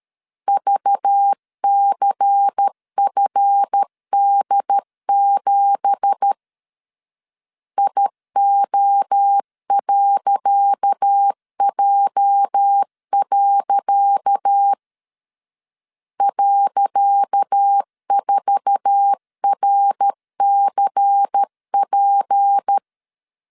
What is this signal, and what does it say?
Morse code, heard as a keyed tone, about 13 wpm, 800 Hz. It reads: VCFD7 IO.J. .4RCP
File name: LEZIONI / 10 SIGNS / 10C DOT 12wpm 2451